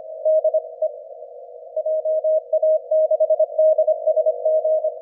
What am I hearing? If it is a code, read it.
Morse code, heard as a keyed tone, about 27 wpm, 620 Hz. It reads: DE JA6DSG
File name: ja6dsg-g8bcg